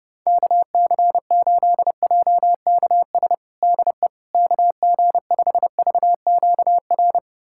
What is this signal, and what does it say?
Morse code, heard as a keyed tone, about 32 wpm, 700 Hz. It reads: KC8JKS DE KG5VQR
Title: USA Callsign DE Callsign - 6 Characters - 30wpm